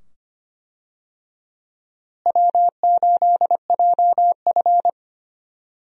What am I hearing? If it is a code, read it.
Morse code, heard as a keyed tone, about 25 wpm, 700 Hz. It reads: W8JF
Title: USA Call Signs - 4 Characters - Repeated 2x - 25wpm